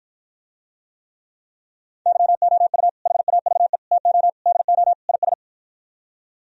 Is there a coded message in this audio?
Yes: Morse code from 2.1 to 5.3 s, heard as a keyed tone, about 52 wpm, 700 Hz. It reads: YOU HAVE TO DO IS